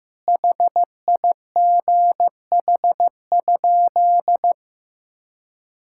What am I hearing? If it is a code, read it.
Morse code, heard as a keyed tone, about 15 wpm, 700 Hz. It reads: HIGH?
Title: Sentences from Top 1000 Words - Review Part 5 - 15wpm